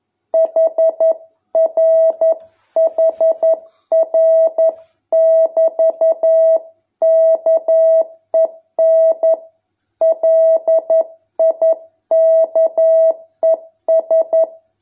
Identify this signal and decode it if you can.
Morse code, heard as a keyed tone, about 11 wpm, 650 Hz. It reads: HRHR<BT>KEN LIKES